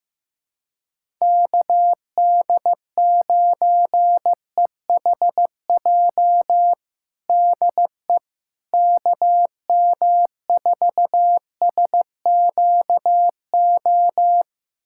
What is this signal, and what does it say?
Morse code, heard as a keyed tone, about 15 wpm, 700 Hz. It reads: KD9EHJ DE KM4SQO
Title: USA Callsign DE Callsign 15wpm